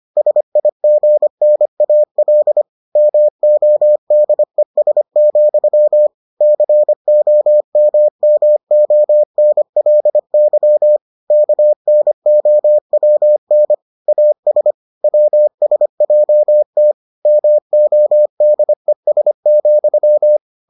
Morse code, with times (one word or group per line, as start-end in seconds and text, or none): 0.2-2.6 SIGNAL
3.0-6.1 MODES,
6.4-11.0 COMMONLY
11.3-13.7 KNOWN
14.1-14.7 AS
15.0-16.9 WSJT
17.3-20.4 MODES,